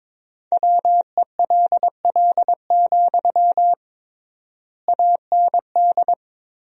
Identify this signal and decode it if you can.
Morse code, heard as a keyed tone, about 22 wpm, 700 Hz. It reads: WELL, AND
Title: 3x Word-Spacing - Sentences from Top 200 Words - 22wpm